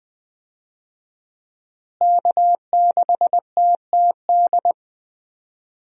Morse code, heard as a keyed tone, about 20 wpm, 700 Hz. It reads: K6TTD
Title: USA Call Signs - 5 Characters - Repeated 2x - 20wpm